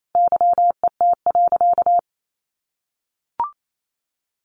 Morse code, heard as a keyed tone, about 28 wpm, 700 Hz. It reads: YET.
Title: A Sound Of Thunder 28wpm